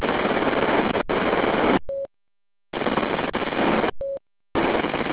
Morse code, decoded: none